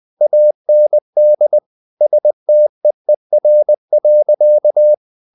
Morse code, heard as a keyed tone, about 20 wpm, 600 Hz. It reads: AND STEER.
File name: the old man and the sea part 2 20wpm